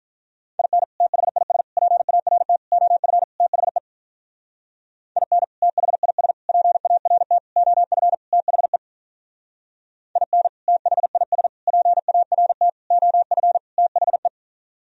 Morse code, heard as a keyed tone, about 44 wpm, 700 Hz. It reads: IN THIS PART OF THE IN THIS PART OF THE IN THIS PART OF THE